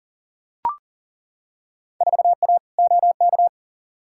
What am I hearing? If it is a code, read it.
Morse code, heard as a keyed tone, about 40 wpm, 700 Hz. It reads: 4A OK
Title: Field Day Exchanges - 40wpm